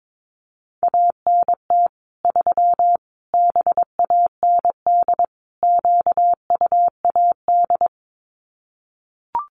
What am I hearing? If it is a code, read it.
Morse code, heard as a keyed tone, about 22 wpm, 700 Hz. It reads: ANT 3 BAND QUAD
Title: Common QSO Phrases 22wpm